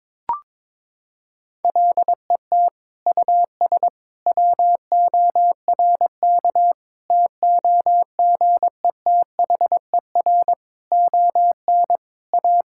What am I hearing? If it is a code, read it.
Morse code, heard as a keyed tone, about 22 wpm, 700 Hz. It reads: LET US WORK TOGETHER ON A
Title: Sentences from Top 1000 Words - Review Part 5 - 22wpm